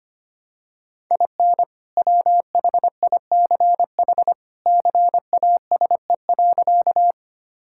Morse code, heard as a keyed tone, about 25 wpm, 700 Hz. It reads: IN WHICH CASE.
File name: practice / 25wpm / Sentences with 2-Word Phrases - 25wpm